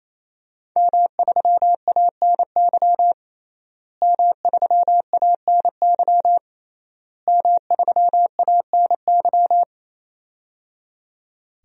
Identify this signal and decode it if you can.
Morse code, heard as a keyed tone, about 28 wpm, 700 Hz. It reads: M3ANY M3ANY M3ANY
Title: DXCC-100 Call Signs - 5 Characters - Repeated 3x - 28wpm